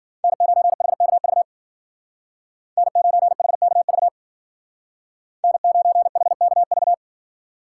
Morse code, heard as a keyed tone, about 52 wpm, 700 Hz. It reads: N9HKV N9HKV N9HKV